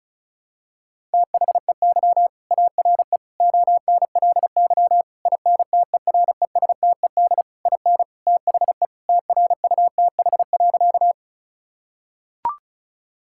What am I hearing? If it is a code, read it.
Morse code, heard as a keyed tone, about 36 wpm, 700 Hz. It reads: THEY ARE ONLY INTERESTED IN THE TRUTH.
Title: Sentences from Top 1000 Words - Review Part 6 - 35wpm